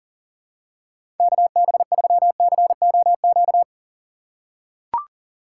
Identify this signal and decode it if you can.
Morse code, heard as a keyed tone, about 40 wpm, 700 Hz. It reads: KB3COQ
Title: USA Call Signs - 4x Character Spacing - 40wpm